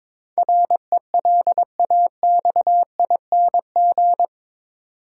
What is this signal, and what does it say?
Morse code, heard as a keyed tone, about 22 wpm, 700 Hz. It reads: RELAXING
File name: practice / 22wpm / Eight Letter Words - 22wpm